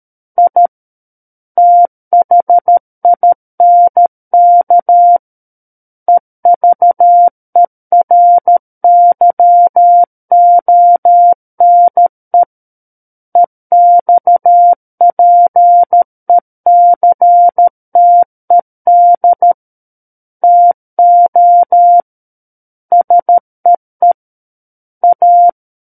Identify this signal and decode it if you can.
Morse code, heard as a keyed tone, about 13 wpm, 700 Hz. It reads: I THINK EVERYONE EXPECTED TO SEE A
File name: Book One / 04-Chapter 4 - 13 WPM